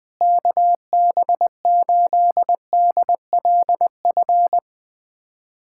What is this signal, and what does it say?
Morse code, heard as a keyed tone, about 20 wpm, 700 Hz. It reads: KB8DLF
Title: USA Callsign DE Callsign 20wpm